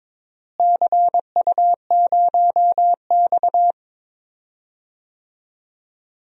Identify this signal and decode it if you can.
Morse code, heard as a keyed tone, about 22 wpm, 700 Hz. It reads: CU0X
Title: DXCC-100 Call Signs - 4 Characters - Repeated 2x - 22wpm